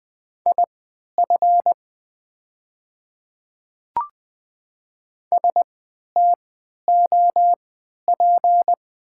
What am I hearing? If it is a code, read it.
Morse code, heard as a keyed tone, about 20 wpm, spaced as for 11 wpm, 700 Hz. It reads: IF STOP
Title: 3x Character Spacing - Top 500 Words - 20wpm